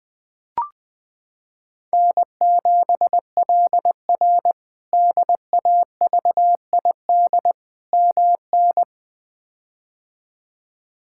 Morse code, heard as a keyed tone, about 20 wpm, 700 Hz. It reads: N7LR DAVID MN
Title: K1USN SST QSO Practice - 20wpm